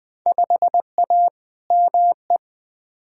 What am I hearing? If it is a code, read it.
Morse code, heard as a keyed tone, about 20 wpm, 700 Hz. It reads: 5A ME
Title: Field Day Exchanges Encore - 20wpm